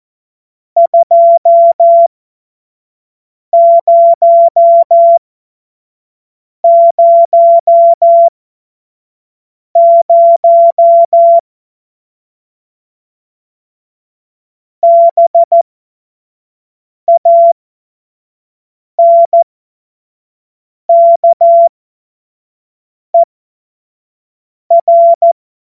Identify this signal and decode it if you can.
Morse code, heard as a keyed tone, about 14 wpm, spaced as for 5 wpm, 680 Hz. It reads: 2000 BANKER
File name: Morsekurs / Witze / Witz 24 Juni 2025 25 F70